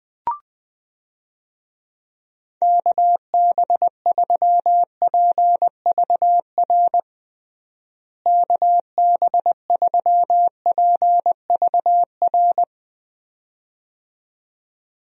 Morse code, heard as a keyed tone, about 20 wpm, 700 Hz. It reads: KB3PVR KB3PVR
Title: USA Call Signs - 6 Characters - Repeated 2x - 20wpm